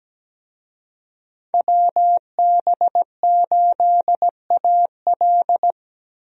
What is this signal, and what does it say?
Morse code, heard as a keyed tone, about 17 wpm, 700 Hz. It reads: WB8AL
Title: USA Call Signs - 5 Characters - Repeated 2x - 17wpm